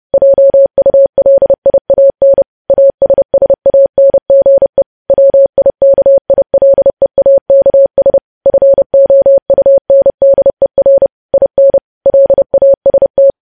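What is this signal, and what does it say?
Morse code, heard as a keyed tone, about 30 wpm, 550 Hz. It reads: JULIAN ASSANGE WIKILEAKS FOUNDER IN LAST
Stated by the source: Morse code news headlines practice at 30 WPM.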